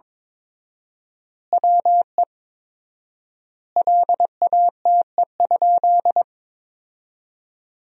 Morse code, heard as a keyed tone, about 22 wpm, 700 Hz. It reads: WE LATE?
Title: 4x Word-Spacing - Sentences from Top 300 Words - 22wpm